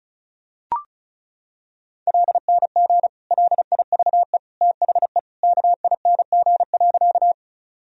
Morse code, heard as a keyed tone, about 36 wpm, 700 Hz. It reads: LNG LIVE THE KING.
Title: Sentences from Top 400 Words and Abbreviations 35wpm